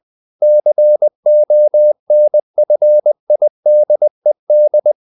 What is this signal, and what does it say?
Morse code, heard as a keyed tone, about 20 wpm, 600 Hz. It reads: CONFIDED
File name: the old man and the sea part 2 20wpm